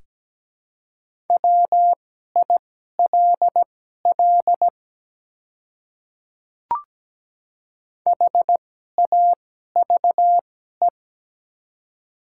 Morse code, heard as a keyed tone, about 17 wpm, spaced as for 12 wpm, 700 Hz. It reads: WILL HAVE
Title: 2x Character Spacing - Top 100 Words - 17wpm